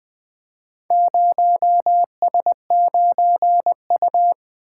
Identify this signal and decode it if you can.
Morse code, heard as a keyed tone, about 20 wpm, 700 Hz. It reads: 0S9U
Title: Four Letter-Number Combinations - 20wpm